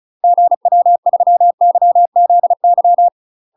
Morse code, heard as a keyed tone, about 34 wpm, 700 Hz. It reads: GW3YZY
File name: UK MASTER QSOs TRAINING / 35WPM / UK MASTER QSOs 0004